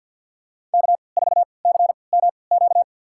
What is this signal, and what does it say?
Morse code, heard as a keyed tone, about 53 wpm, spaced as for 29 wpm, 700 Hz. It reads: K4CMQ
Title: USA Call Signs - 3x Character Spacing - 50wpm